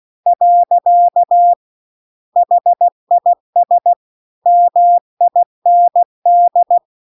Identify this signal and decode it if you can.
Morse code, heard as a keyed tone, about 16 wpm, 700 Hz. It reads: . HIS MIND